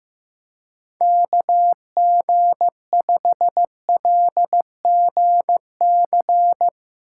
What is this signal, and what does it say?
Morse code, heard as a keyed tone, about 16 wpm, 700 Hz. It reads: KG5LGC
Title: Speed Racing - USA 2x3 Call Signs 15wpm